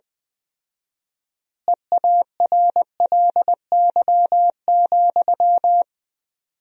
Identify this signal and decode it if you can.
Morse code, heard as a keyed tone, about 20 wpm, 700 Hz. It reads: EARLY,
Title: 4x Word-Spacing - Sentences from Top 400 Words - 20wpm